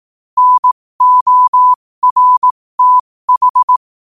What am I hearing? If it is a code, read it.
Morse code, heard as a keyed tone, about 18 wpm, spaced as for 15 wpm, 1000 Hz. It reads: NORTH
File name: qotdfiles / 15wpm